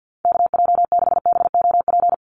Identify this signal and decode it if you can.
Morse code, heard as a keyed tone, about 50 wpm, 700 Hz. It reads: KJ6BOP
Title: USA Call Signs 50wpm